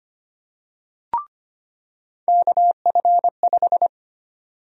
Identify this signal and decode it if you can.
Morse code, heard as a keyed tone, about 25 wpm, 700 Hz. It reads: KF5
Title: USA Call Sign Prefixes - 25wpm